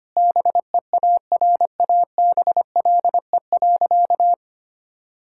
Morse code, heard as a keyed tone, about 25 wpm, 700 Hz. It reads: BEARABLE.
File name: practice / 25wpm / Difficult Sentences with Hints - 25wpm